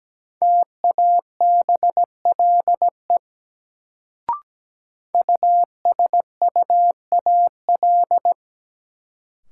Morse code, heard as a keyed tone, about 17 wpm, 700 Hz. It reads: TABLE USUAL
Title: Sentences from Top 400 Words - New Word Drill - 17wpm